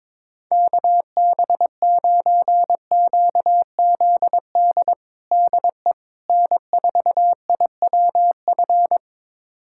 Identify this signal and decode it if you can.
Morse code, heard as a keyed tone, about 22 wpm, 700 Hz. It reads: KB9QZD DE N4IWF
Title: USA Callsign DE Callsign 22wpm